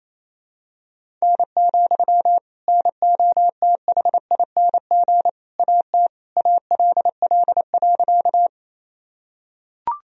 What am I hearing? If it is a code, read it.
Morse code, heard as a keyed tone, about 28 wpm, 700 Hz. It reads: N, NOTHING AT ALL.
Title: Sentences from Top 500 Words and Abbreviations 28wpm